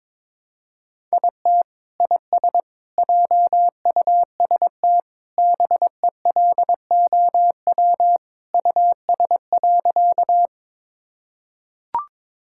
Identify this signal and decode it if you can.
Morse code, heard as a keyed tone, about 22 wpm, 700 Hz. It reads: IT IS JUST BELOW US.
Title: Sentences from Top 1000 Words - Review Part 5 - 22wpm